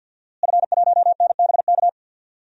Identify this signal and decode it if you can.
Morse code, heard as a keyed tone, about 52 wpm, 700 Hz. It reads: F1NBK
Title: DXCC-100 Call Signs - 5 Characters - Repeated 2x - 50wpm